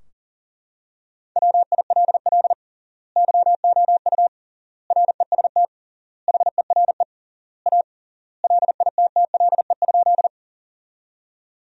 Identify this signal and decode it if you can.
Morse code, heard as a keyed tone, about 40 wpm, 700 Hz. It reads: WILL YOU REST HERE A LITTLE?
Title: 3x Word-Spacing - Sentences from Top 500 Words - 40wpm